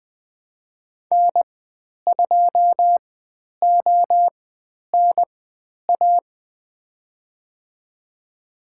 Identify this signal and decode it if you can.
Morse code, heard as a keyed tone, about 20 wpm, spaced as for 10 wpm, 700 Hz. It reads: N2ONA
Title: 102 Lesson 17g - Letter u - Call Signs 20wpm